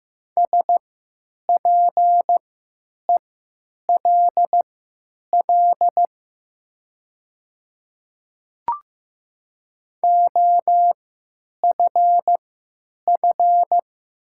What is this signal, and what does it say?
Morse code, heard as a keyed tone, about 15 wpm, spaced as for 9 wpm, 700 Hz. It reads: SPELL OFF
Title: 3x Character Spacing - Top 500 Words - 15wpm